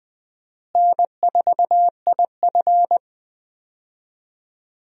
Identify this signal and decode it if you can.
Morse code, heard as a keyed tone, about 20 wpm, 700 Hz. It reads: N4IF
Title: USA Call Signs - 4 Characters - Repeated 2x - 20wpm